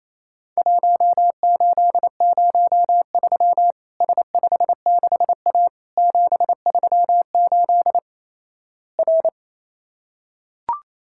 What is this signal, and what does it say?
Morse code, heard as a keyed tone, about 28 wpm, 700 Hz. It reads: 1803 S56A 738 R
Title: Worked All Europe DX Contest - QTC Exchange - 28wpm